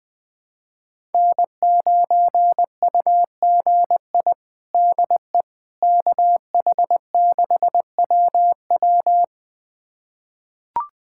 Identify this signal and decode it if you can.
Morse code, heard as a keyed tone, about 21 wpm, 700 Hz. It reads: N9UGI DE KH6WW
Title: USA Callsign DE Callsign - 5 Characters - 20wpm